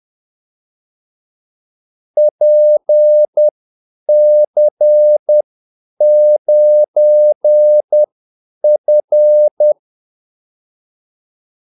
Morse code, heard as a keyed tone, about 10 wpm, spaced as for 8 wpm, 600 Hz. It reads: PC9F
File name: Callsigns-10-8wpm